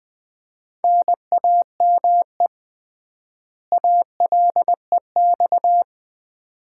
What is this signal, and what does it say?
Morse code, heard as a keyed tone, about 19 wpm, 700 Hz. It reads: NAME ALEX
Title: Common QSO Phrases - 3x Word-Spacing - 20wpm